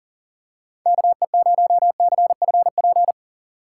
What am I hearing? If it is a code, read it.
Morse code, heard as a keyed tone, about 40 wpm, 700 Hz. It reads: KE0CFP
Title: USA Call Signs - 6 Characters - 3x Character Spacing - 40wpm